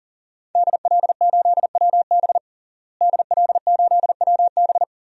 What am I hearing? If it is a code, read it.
Morse code, heard as a keyed tone, about 40 wpm, 700 Hz. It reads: DL8WB DL8WB
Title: DXCC-100 Call Signs - 5 Characters - Repeated 2x - 40wpm